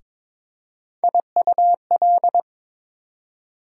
Morse code, heard as a keyed tone, about 22 wpm, 700 Hz. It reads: IUL
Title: USA Call Sign Postfixes - 3 Letters 22wpm